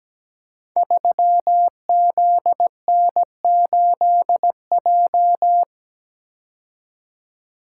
Morse code, heard as a keyed tone, about 17 wpm, 700 Hz. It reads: 3ZN8J
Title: Code Groups - 5 Letter:Numbers - 17wpm